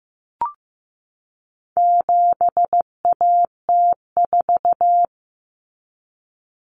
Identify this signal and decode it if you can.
Morse code, heard as a keyed tone, about 15 wpm, 700 Hz. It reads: 7AT4
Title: Common Cut Numbers - 4-Digit 15wpm